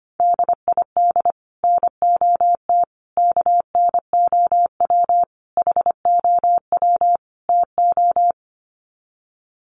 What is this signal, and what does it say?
Morse code, heard as a keyed tone, about 25 wpm, 700 Hz. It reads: DID NOT KNOW HOW TO
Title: Sets of 5 Words 25wpm